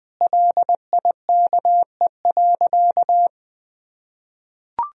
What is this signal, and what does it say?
Morse code, heard as a keyed tone, about 21 wpm, 700 Hz. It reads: LIKE.
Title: Sentences from Top 500 Words and Abbreviations 20wpm